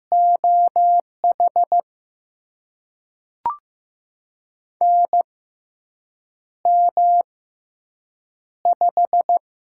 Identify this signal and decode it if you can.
Morse code, heard as a keyed tone, about 15 wpm, spaced as for 7 wpm, 700 Hz. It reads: OH NM5
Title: USA Call Signs - 4 Characters - 6x Character Spacing - 15wpm